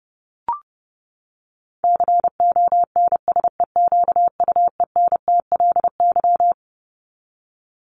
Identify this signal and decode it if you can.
Morse code, heard as a keyed tone, about 30 wpm, 700 Hz. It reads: CONSEQUENTLY
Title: LY Suffix - Top 100 Words with Base Words 30wpm